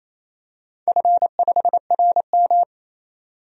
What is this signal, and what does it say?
Morse code, heard as a keyed tone, about 28 wpm, 700 Hz. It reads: F5RM
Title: DXCC-100 Call Signs - 4 Characters - 28wpm